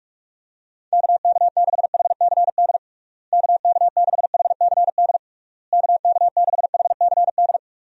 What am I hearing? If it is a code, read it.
Morse code, heard as a keyed tone, about 44 wpm, 700 Hz. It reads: KK6HCD KK6HCD KK6HCD
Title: USA Call Signs - 6 Characters - Repeated 3x - 45wpm